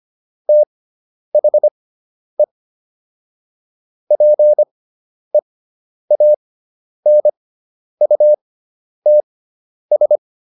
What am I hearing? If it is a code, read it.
Morse code, heard as a keyed tone, about 25 wpm, spaced as for 10 wpm, 600 Hz. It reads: THE PEANUTS